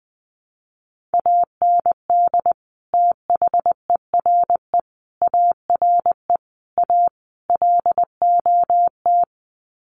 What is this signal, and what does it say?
Morse code, heard as a keyed tone, about 20 wpm, 700 Hz. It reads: AND THERE ARE A LOT
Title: Sets of 5 Words Encore2 20wpm